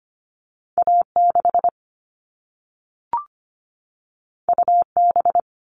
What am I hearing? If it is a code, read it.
Morse code, heard as a keyed tone, about 25 wpm, 700 Hz. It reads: A6 UB
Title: All Cut Numbers - 2-Digit 25wpm